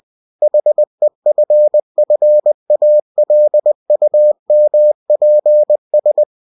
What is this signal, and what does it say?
Morse code, heard as a keyed tone, about 20 wpm, 600 Hz. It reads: HEFFALUMPS
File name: cw-books / winnie-the-pooh / winnie-20wpm-page-0037